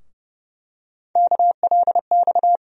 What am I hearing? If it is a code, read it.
Morse code, heard as a keyed tone, about 30 wpm, 700 Hz. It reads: KLX
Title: USA Call Sign Postfixes - 3 Letters 30wpm